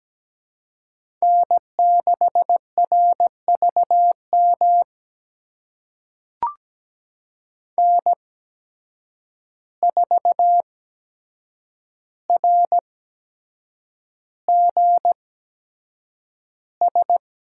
Morse code, heard as a keyed tone, about 17 wpm, spaced as for 7 wpm, 700 Hz. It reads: N6RVM N4RGS
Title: USA Call Signs - 5 Characters - 8x Character Spacing - 17wpm